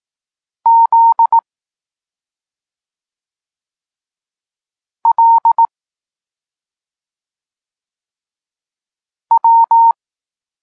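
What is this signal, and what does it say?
Morse code, heard as a keyed tone, about 18 wpm, 920 Hz. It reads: Z L W
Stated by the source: Morse code lesson for learning the letters K and X, from the SP9MOA Niepołomicki Klub Krótkofalowców course.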